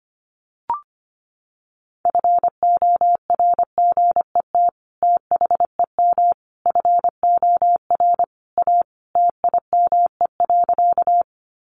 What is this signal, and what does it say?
Morse code, heard as a keyed tone, about 25 wpm, 700 Hz. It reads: FORGET THEM FOR A TIME.
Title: Sentences from Top 1200 Words 25wpm